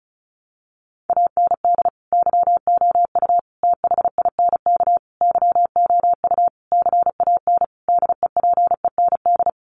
Morse code, heard as a keyed tone, about 34 wpm, 700 Hz. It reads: AND YOU THINK YOU CAN DEPEND
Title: Sentences from Top 1300 Words 35wpm